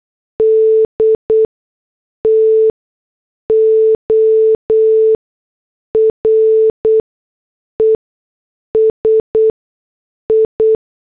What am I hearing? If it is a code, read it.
Morse code, heard as a keyed tone, about 8 wpm, spaced as for 6 wpm, 440 Hz. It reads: D TO RESI